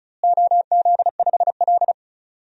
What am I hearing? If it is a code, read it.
Morse code, heard as a keyed tone, about 34 wpm, 700 Hz. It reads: OZ5L
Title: DXCC-100 Call Signs - 4 Characters - Repeated 3x - 35wpm